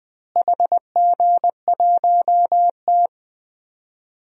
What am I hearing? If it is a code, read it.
Morse code, heard as a keyed tone, about 21 wpm, 700 Hz. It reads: HG1T
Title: DXCC-100 Call Signs - 4 Characters - Repeated 3x - 20wpm